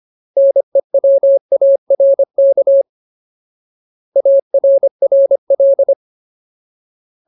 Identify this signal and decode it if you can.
Morse code, heard as a keyed tone, about 25 wpm, 550 Hz. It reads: NEWARK ARRL
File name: Ultimate Word List / 25WPM Ultimate